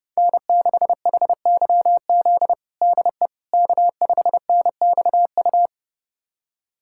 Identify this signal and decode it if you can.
Morse code, heard as a keyed tone, about 32 wpm, 700 Hz. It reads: N6HYZ DE K5NXU
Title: USA Callsign DE Callsign - 5 Characters - 30wpm